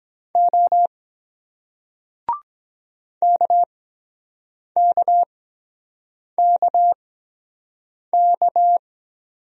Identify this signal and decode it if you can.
Morse code, heard as a keyed tone, about 23 wpm, spaced as for 7 wpm, 700 Hz. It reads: O KKKK